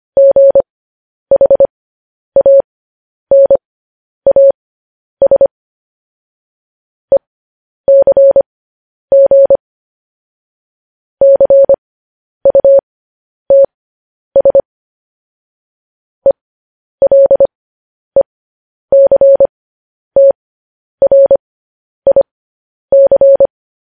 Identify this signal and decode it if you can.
Morse code, heard as a keyed tone, about 25 wpm, spaced as for 10 wpm, 550 Hz. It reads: GHANAS ECG CUTS ELECTRIC
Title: News Headlines at 10